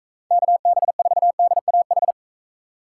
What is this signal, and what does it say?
Morse code, heard as a keyed tone, about 38 wpm, 700 Hz. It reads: KB4DAH